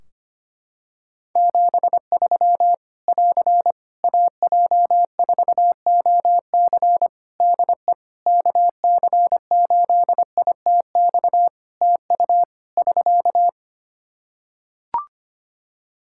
Morse code, heard as a keyed tone, about 25 wpm, 700 Hz. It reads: 73 <AR> AJ4OC DE KC8ITX TU <SK>
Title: Practice QSO - Exchange 4 - Ending - 25wpm